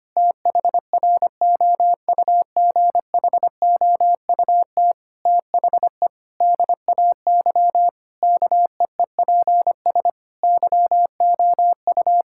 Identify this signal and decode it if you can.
Morse code, heard as a keyed tone, about 25 wpm, 700 Hz. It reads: THROUGHOUT THE DAY KEEPS YOU
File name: practice / 25wpm / Difficult Sentences with Hints - 25wpm